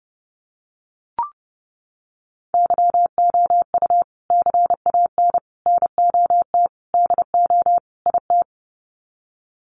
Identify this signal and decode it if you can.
Morse code, heard as a keyed tone, about 30 wpm, 700 Hz. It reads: YOU CAN NOT DO IT
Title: Sets of 5 Words Encore2 30wpm